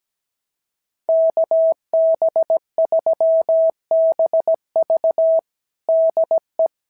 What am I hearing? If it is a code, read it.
Morse code, heard as a keyed tone, about 17 wpm, 660 Hz. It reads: KB3BV DE
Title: SOTA QSO Practice 17wpm